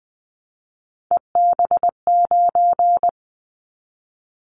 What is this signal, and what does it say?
Morse code, heard as a keyed tone, about 20 wpm, 700 Hz. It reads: EB9
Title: Callsign Prefixes for Top 100 DXCC - 20wpm